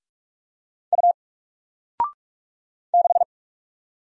Morse code, heard as a keyed tone, about 46 wpm, 700 Hz. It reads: U 6
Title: Single Letter-Number Alt 45wpm